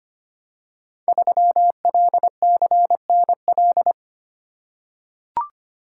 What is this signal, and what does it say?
Morse code, heard as a keyed tone, about 25 wpm, 700 Hz. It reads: 3LCNL
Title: Code Groups - 5 Letter:Numbers - 25wpm